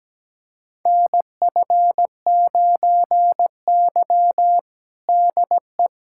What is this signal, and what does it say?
Morse code, heard as a keyed tone, about 17 wpm, 700 Hz. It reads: NF9Y DE
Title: USA Callsign DE Callsign - 4 Characters - 17wpm